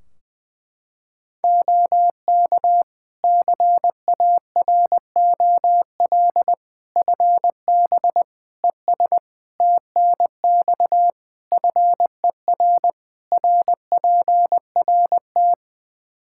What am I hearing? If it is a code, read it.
Morse code, heard as a keyed tone, about 20 wpm, 700 Hz. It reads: OK CAROL FB ES TNX FER RPRT
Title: K1USN SST QSO Practice 20wpm